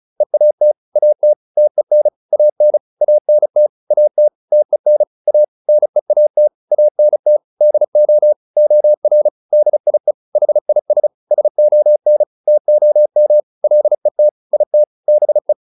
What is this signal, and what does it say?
Morse code, heard as a keyed tone, about 34 wpm, 600 Hz. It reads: EAT AT TEN AN ANT AT TEN A NEAT ANT DO OR DIE HIS SON TOM LET IT BE